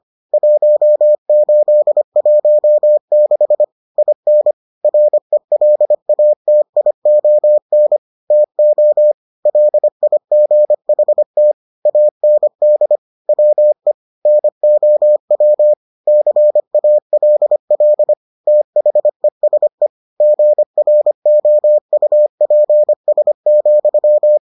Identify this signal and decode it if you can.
Morse code, heard as a keyed tone, about 25 wpm, 600 Hz. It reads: 1816 IN RELATION TO LIGHT AND WE NOW CALL THESE GROUPS,